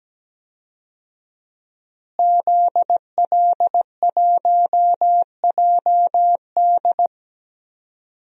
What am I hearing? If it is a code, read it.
Morse code, heard as a keyed tone, about 17 wpm, 700 Hz. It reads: ZL1JD
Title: DXCC-100 Call Signs - 5 Characters - Repeated 3x - 17wpm